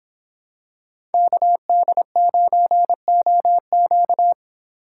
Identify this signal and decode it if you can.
Morse code, heard as a keyed tone, about 26 wpm, 700 Hz. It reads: KD9OQ